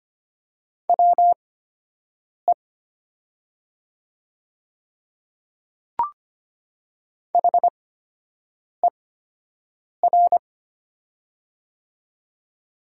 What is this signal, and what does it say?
Morse code, heard as a keyed tone, about 25 wpm, spaced as for 7 wpm, 700 Hz. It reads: WE HER
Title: 8x Character Spacing - Top 100 Words - 25wpm